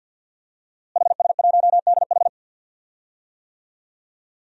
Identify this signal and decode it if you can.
Morse code, heard as a keyed tone, about 51 wpm, 700 Hz. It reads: HS1DH
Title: DXCC-100 Call Signs - 5 Characters - Repeated 2x - 50wpm